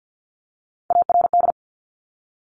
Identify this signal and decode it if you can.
Morse code, heard as a keyed tone, about 50 wpm, 690 Hz. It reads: ARD
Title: Top Three Letter Combinations in Words 50wpm